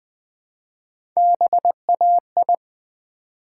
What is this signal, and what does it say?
Morse code, heard as a keyed tone, about 20 wpm, 700 Hz. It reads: BAI
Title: Build A Word - 1 Letter At A Time - 3 to 5-Letter Words - 20wpm